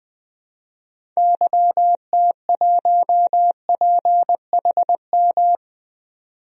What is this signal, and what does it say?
Morse code, heard as a keyed tone, about 20 wpm, 700 Hz. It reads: YT1PHM
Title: DXCC-100 Call Signs - 6 Characters - 20wpm